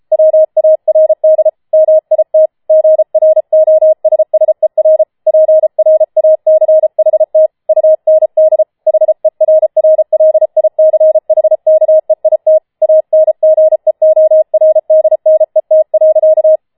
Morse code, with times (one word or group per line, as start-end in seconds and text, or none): 0.1-1.5 WARD
1.7-2.5 MIT
2.7-5.0 GROSSER
5.3-7.5 PRACHT
7.7-8.6 UND
8.9-12.6 HERRLICHKEIT
12.8-16.6 ANGEORDNET.